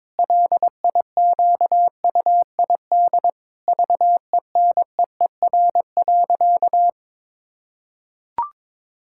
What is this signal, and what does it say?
Morse code, heard as a keyed tone, about 22 wpm, 700 Hz. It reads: LIQUID VENEER.
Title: Pangram Challenge v2 - 22wpm